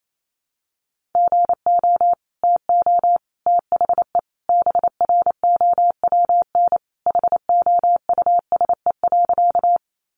Morse code, heard as a keyed tone, about 28 wpm, 700 Hz. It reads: GO TO THE BROWN HOUSE.
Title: Sentences from Top 1200 Words 28wpm